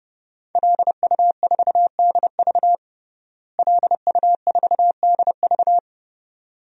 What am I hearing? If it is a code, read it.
Morse code, heard as a keyed tone, about 30 wpm, 700 Hz. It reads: LU4DV LU4DV
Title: DXCC-100 Call Signs - 5 Characters - Repeated 2x - 30wpm